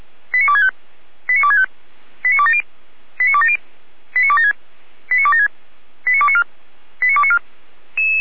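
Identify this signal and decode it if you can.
Five-tone selective call (ZVEI), heard as quick runs of five tones, tones of about 70 ms. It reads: 89265 89265 89270 89270 89276 89276 89294 89294
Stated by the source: ZVEI five-tone selective call received by SDR, recorded on the audio server of Feuerwehr Orlamünde.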